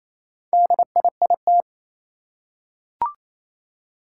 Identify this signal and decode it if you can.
Morse code, heard as a keyed tone, about 27 wpm, 700 Hz. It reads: DIIT
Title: Four Challenging Letter-Number Combinations 28wpm